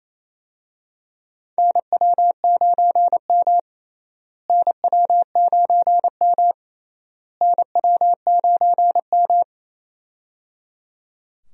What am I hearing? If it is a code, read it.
Morse code, heard as a keyed tone, about 28 wpm, 700 Hz. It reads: NW9M NW9M NW9M
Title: USA Call Signs - 4 Characters - Repeated 3x - 28wpm